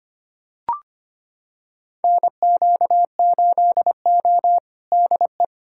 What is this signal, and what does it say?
Morse code, heard as a keyed tone, about 25 wpm, 700 Hz. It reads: NQ8O DE
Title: USA Callsign DE Callsign - 4 Characters - 25wpm